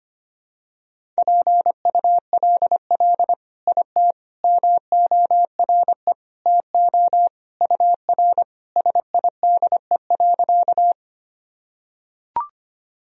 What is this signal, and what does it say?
Morse code, heard as a keyed tone, about 25 wpm, 700 Hz. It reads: PULL IT MORE TO UR SIDE.
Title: Sentences from Top 400 Words and Abbreviations 25wpm